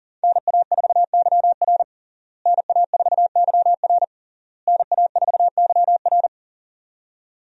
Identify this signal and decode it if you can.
Morse code, heard as a keyed tone, about 40 wpm, 690 Hz. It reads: NA4YR NA4YR NA4YR
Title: USA Call Signs - 5 Characters - Repeated 3x - 40wpm